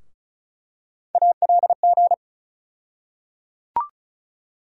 Morse code, heard as a keyed tone, about 36 wpm, 700 Hz. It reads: ALG
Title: USA Call Sign Postfixes - 3 Letters 35wpm